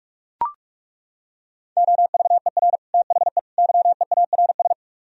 Morse code, heard as a keyed tone, about 36 wpm, 700 Hz. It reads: OVER THE YEARS
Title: Speed-Racing - Sets of 3 Words 30wpm